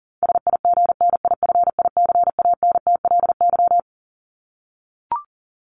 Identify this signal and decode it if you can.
Morse code, heard as a keyed tone, about 40 wpm, 700 Hz. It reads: SIGNIFICANTLY
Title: LY Suffix - Top 100 Words with Base Words 40wpm